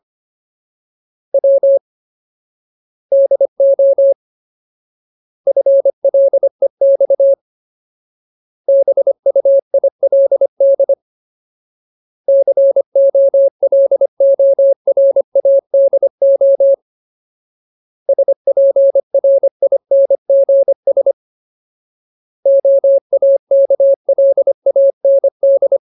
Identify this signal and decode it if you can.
Morse code, heard as a keyed tone, about 25 wpm, 550 Hz. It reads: W DO FLEX BUILD COLORADO SPRINGS OAKLAND
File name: Ultimate Word List / 25WPM Ultimate